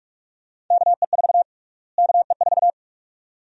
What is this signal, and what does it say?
Morse code, heard as a keyed tone, about 47 wpm, 700 Hz. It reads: KE4 KE4
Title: USA Call Sign Prefixes - Repeated 2x - 45wpm